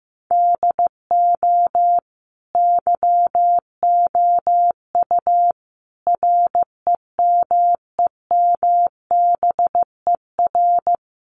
Morse code, heard as a keyed tone, about 15 wpm, 700 Hz. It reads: DO YOU REMEMBER
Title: Sentences from Top 700 Words 15wpm